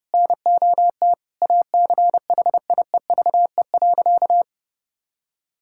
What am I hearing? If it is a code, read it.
Morse code, heard as a keyed tone, about 29 wpm, 700 Hz. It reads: NOT ACHIEVE.
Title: Sentences from Top 1000 Words - Review Part 6 - 30wpm